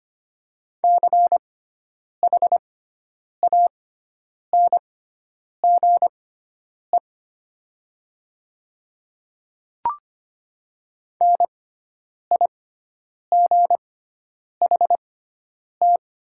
Morse code, heard as a keyed tone, about 25 wpm, spaced as for 9 wpm, 700 Hz. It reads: CHANGE NIGHT
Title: 6x Character Spacing - Top 300 Words - 25wpm